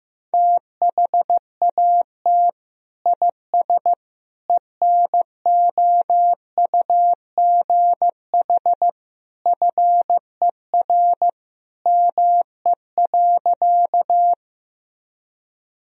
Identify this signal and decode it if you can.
Morse code, heard as a keyed tone, about 15 wpm, 700 Hz. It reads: THAT IS ENOUGH FER ME.
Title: Sentences from Top 400 Words and Abbreviations 15wpm